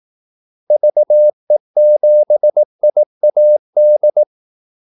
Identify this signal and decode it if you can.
Morse code, heard as a keyed tone, about 18 wpm, 600 Hz. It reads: VE7IAD
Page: Kaldesignaler 18 wpm